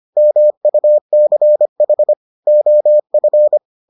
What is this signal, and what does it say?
Morse code, heard as a keyed tone, about 25 wpm, 600 Hz. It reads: MUCH OF